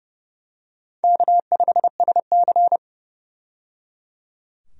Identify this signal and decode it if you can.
Morse code, heard as a keyed tone, about 29 wpm, 700 Hz. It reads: K5SC
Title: USA Call Signs - 4 Characters - 30wpm